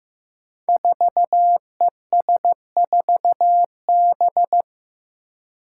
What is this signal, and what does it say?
Morse code, heard as a keyed tone, about 15 wpm, 700 Hz. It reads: 4ES4B
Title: Five Challenging Letter-Number Combinations 15wpm